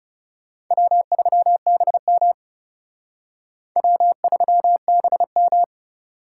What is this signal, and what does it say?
Morse code, heard as a keyed tone, about 37 wpm, 700 Hz. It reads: W3BM W3BM
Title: Speed Racing - USA 1x2 Call Signs 30wpm